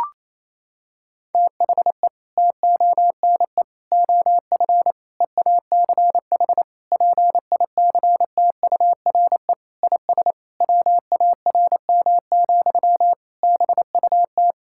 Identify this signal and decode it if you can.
Morse code, heard as a keyed tone, about 28 wpm, 700 Hz. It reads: THE TONE OF EACH PICTURE IS WARM, BUT
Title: Sentences from Top 1600 Words 28wpm